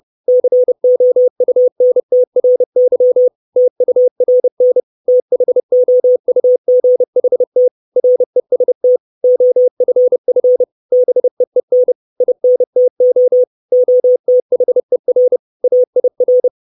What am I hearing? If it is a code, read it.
Morse code, heard as a keyed tone, about 30 wpm, 500 Hz. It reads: COUNTRY TURN THOUGHT REST OFF BEEN INTO OTHER AIR
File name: 632 Words 30WPM